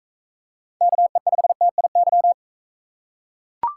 Morse code, heard as a keyed tone, about 41 wpm, 700 Hz. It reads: KE5TIY